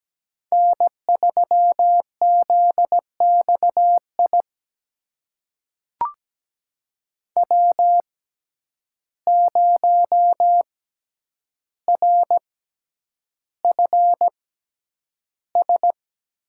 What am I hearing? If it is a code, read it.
Morse code, heard as a keyed tone, about 17 wpm, spaced as for 9 wpm, 700 Hz. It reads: N3ZXI W0RFS
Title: USA Call Signs - 5 Characters - 6x Character Spacing - 17wpm